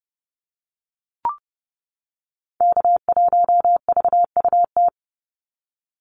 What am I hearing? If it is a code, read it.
Morse code, heard as a keyed tone, about 30 wpm, 700 Hz. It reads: K1VUT
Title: Call Signs - 30wpm